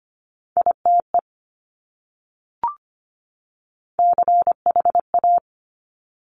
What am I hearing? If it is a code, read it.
Morse code, heard as a keyed tone, about 25 wpm, 700 Hz. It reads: ITE CHA
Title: Top Three Letter Combinations in Words 25wpm